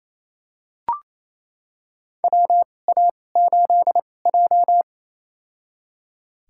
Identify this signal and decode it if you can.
Morse code, heard as a keyed tone, about 28 wpm, spaced as for 20 wpm, 700 Hz. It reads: WA8J
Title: USA Call Signs - 4 Characters - 2x Character Spacing - 28wpm